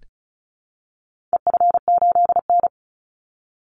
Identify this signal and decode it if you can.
Morse code, heard as a keyed tone, about 34 wpm, 700 Hz. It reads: EF8N
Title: Call Signs Repeated 3x - 35wpm